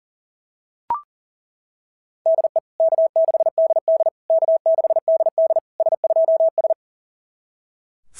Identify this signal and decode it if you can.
Morse code, heard as a keyed tone, about 40 wpm, 690 Hz. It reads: DE K6DD K6DD S2S
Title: SOTA QSO Practice 40wpm